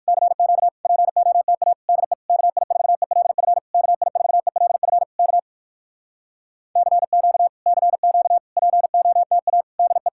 Morse code, heard as a keyed tone, about 58 wpm, 700 Hz. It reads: CQ POTA DE KI4ELF KI4ELF K CQ CQ POTA DE